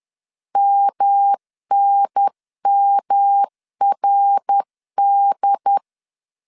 Morse code, heard as a keyed tone, about 10 wpm, 800 Hz. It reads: MNMRD
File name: 10wpm